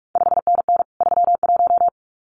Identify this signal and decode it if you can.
Morse code, heard as a keyed tone, about 46 wpm, 700 Hz. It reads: 5NN 31
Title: CQ WW DX Contest Exchange Encore - 45wpm